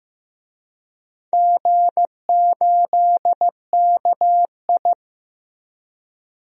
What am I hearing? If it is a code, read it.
Morse code, heard as a keyed tone, about 15 wpm, 700 Hz. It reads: G8KI
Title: DXCC-100 Call Signs - 4 Characters - Repeated 2x - 15wpm